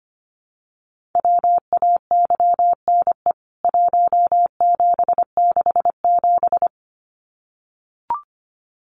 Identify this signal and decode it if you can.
Morse code, heard as a keyed tone, about 25 wpm, 700 Hz. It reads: WAYNE 1767
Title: CWT Exchange 25wpm